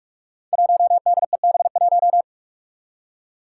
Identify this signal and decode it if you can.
Morse code, heard as a keyed tone, about 42 wpm, 700 Hz. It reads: 1DEB1
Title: Five Challenging Letter-Number Combinations 45wpm